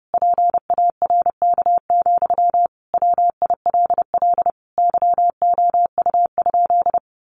Morse code, heard as a keyed tone, about 30 wpm, 700 Hz. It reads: PARK, WILL YOU?
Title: Sentences from Top 1400 Words 30wpm